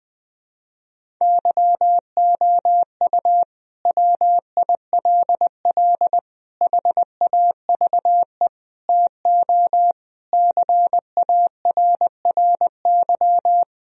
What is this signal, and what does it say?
Morse code, heard as a keyed tone, about 20 wpm, 700 Hz. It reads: YOU WILL HAVE TO CARRY
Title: Sentences from Top 1000 Words - Review Part 1 - 20wpm